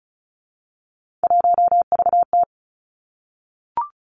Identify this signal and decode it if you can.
Morse code, heard as a keyed tone, about 36 wpm, 700 Hz. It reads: 1VT
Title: Three Challenging Letter-Number Combinations 35wpm